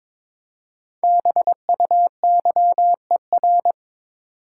Morse code, heard as a keyed tone, about 23 wpm, 700 Hz. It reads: BUYER
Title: Five Letter Words - 22wpm